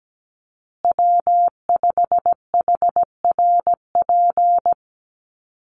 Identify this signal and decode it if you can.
Morse code, heard as a keyed tone, about 17 wpm, 700 Hz. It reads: W5HRP
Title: USA Call Signs Repeated 2x 17wpm